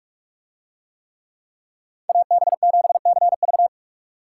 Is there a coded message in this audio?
Yes: Morse code from 2.1 to 3.7 s, heard as a keyed tone, about 44 wpm, 700 Hz. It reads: AB7CV